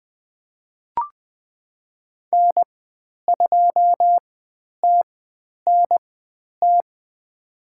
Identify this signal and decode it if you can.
Morse code, heard as a keyed tone, about 20 wpm, spaced as for 10 wpm, 700 Hz. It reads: N2TNT
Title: 095 Lesson 16g - Letter c - Call Signs 20wpm